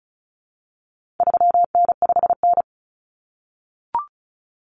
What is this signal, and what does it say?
Morse code, heard as a keyed tone, about 34 wpm, 700 Hz. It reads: 3N5N
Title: Common Cut Numbers - 4-Digit 35wpm